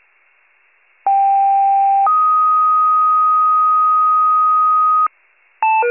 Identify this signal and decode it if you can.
Two-tone page, heard as a one-second tone, then a three-second tone, about 790 then 1250 Hz. It is heard from 1.1 to 5.1 s.